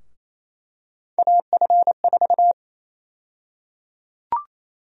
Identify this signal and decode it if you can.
Morse code, heard as a keyed tone, about 28 wpm, 700 Hz. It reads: AF4
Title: USA Call Sign Prefixes - Repeated 2x - 28wpm